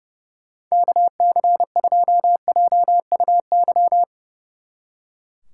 Morse code, heard as a keyed tone, about 37 wpm, 700 Hz. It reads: KC2JUY
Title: Speed Racing - USA 2x3 Call Signs 30wpm